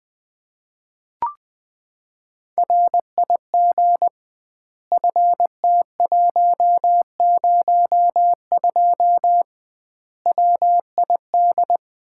Morse code, heard as a keyed tone, about 20 wpm, 700 Hz. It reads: RIG FT102 WID
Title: Common QSO Phrases - 2x Word-Spacing - 20wpm